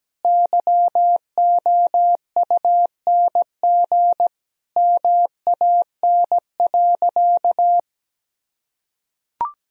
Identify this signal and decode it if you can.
Morse code, heard as a keyed tone, about 17 wpm, 700 Hz. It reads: YOUNG MAN.
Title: Sentences from Top 400 Words and Abbreviations 17wpm